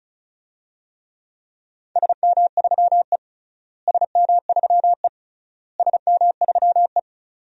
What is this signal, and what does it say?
Morse code, heard as a keyed tone, about 34 wpm, 700 Hz. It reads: SM3E SM3E SM3E
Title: DXCC-100 Call Signs - 4 Characters - Repeated 3x - 35wpm